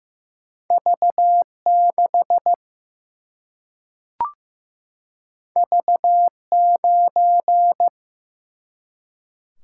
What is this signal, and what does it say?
Morse code, heard as a keyed tone, about 15 wpm, 700 Hz. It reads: V 6 V 9
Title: Single-Letter Single-Number 15wpm